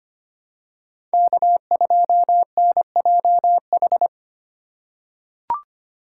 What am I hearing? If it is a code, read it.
Morse code, heard as a keyed tone, about 25 wpm, 700 Hz. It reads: K2NJH
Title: USA Call Signs - 2x Character Spacing - 25wpm